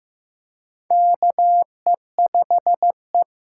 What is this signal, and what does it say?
Morse code, heard as a keyed tone, about 15 wpm, 700 Hz. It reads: KE5E
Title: USA Call Signs - 6 Characters - 6x Character Spacing - 15wpm